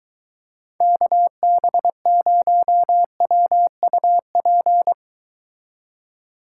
Morse code, heard as a keyed tone, about 23 wpm, 700 Hz. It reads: KB0WUP